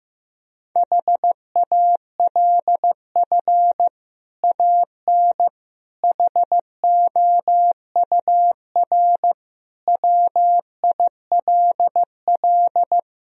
Morse code, heard as a keyed tone, about 15 wpm, 700 Hz. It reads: HALF AN HOUR WILL
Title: Sentences from Top 1000 Words - Review Part 1 - 15wpm